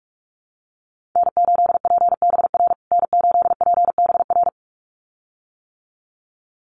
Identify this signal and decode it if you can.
Morse code, heard as a keyed tone, about 44 wpm, 700 Hz. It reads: N8PBR N8PBR
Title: USA Call Signs Repeated 2x 45wpm